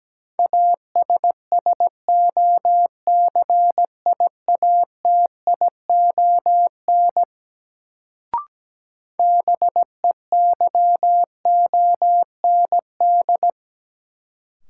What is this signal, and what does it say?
Morse code, heard as a keyed tone, about 17 wpm, 700 Hz. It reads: ASSOCIATION BEYOND
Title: Sentences from Top 1100 Words - New Word Drill Alt - 17wpm